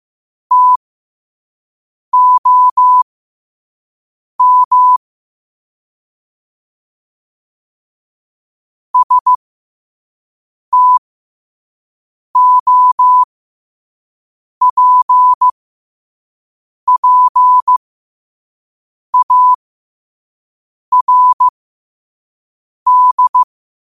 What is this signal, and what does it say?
Morse code, heard as a keyed tone, about 15 wpm, spaced as for 5 wpm, 1000 Hz. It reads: TOM STOPPARD
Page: New quotes every day in morse code at 5 Words per minute.